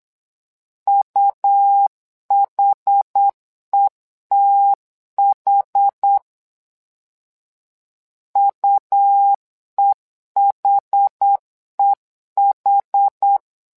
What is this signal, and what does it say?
Morse code, heard as a keyed tone, about 8 wpm, 800 Hz. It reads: UHETH UEHEH
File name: LEZIONI / 2 UHDN / 2 UH 8wpm 2129